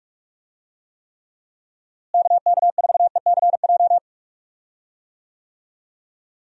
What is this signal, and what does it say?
Morse code, heard as a keyed tone, about 42 wpm, 700 Hz. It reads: KK4ECJ
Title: USA Call Signs - 6 Characters - Repeated 3x - 45wpm